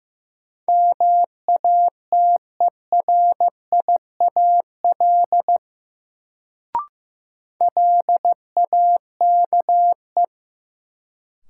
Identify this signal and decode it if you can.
Morse code, heard as a keyed tone, about 15 wpm, 700 Hz. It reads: MATERIAL LAKE
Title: Sentences from Top 1100 Words - New Word Drill Alt - 15wpm